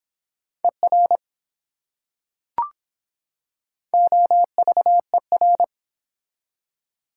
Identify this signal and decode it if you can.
Morse code, heard as a keyed tone, about 24 wpm, 700 Hz. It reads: ER OVER